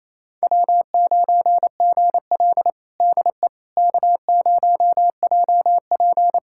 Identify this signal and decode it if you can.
Morse code, heard as a keyed tone, about 27 wpm, 700 Hz. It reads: W9GL DE K0JP
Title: Practice QSO - Exchange 4 - Ending - 28wpm